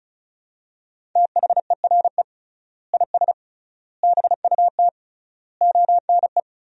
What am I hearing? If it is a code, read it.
Morse code, heard as a keyed tone, about 36 wpm, 700 Hz. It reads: THERE IS BUT ONE
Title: 3x Word-Spacing - Sentences from Top 100 Words - 35wpm